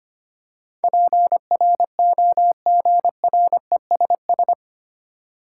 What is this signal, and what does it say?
Morse code, heard as a keyed tone, about 25 wpm, 700 Hz. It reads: PROGRESS
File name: 8 Letter Words - First 3 Letters Given - 25wpm